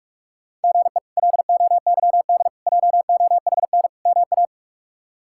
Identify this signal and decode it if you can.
Morse code, heard as a keyed tone, about 46 wpm, 700 Hz. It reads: GE LOYD JOHN MA
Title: K1USN SST QSO Practice - 45wpm